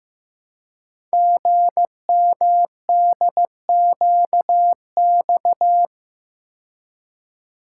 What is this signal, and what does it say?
Morse code, heard as a keyed tone, about 15 wpm, 700 Hz. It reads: GMDQX
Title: Code Groups - 5 Letters - 15wpm